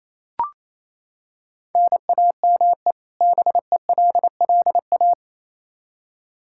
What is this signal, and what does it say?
Morse code, heard as a keyed tone, about 28 wpm, 700 Hz. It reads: NAME BELLA
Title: Practice QSO - Exchange 1 - Beginning - 28wpm